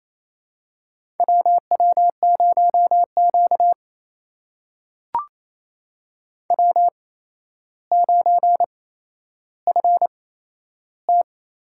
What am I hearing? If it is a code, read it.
Morse code, heard as a keyed tone, about 28 wpm, spaced as for 12 wpm, 700 Hz. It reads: WW0Q W9FT
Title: USA Call Signs - 8x Character Spacing - 28wpm